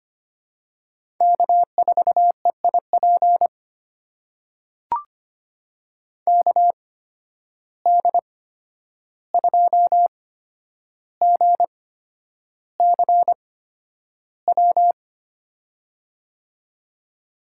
Morse code, heard as a keyed tone, about 25 wpm, spaced as for 10 wpm, 700 Hz. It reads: K4EIP KD2GCW
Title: USA Call Signs - 8x Character Spacing - 25wpm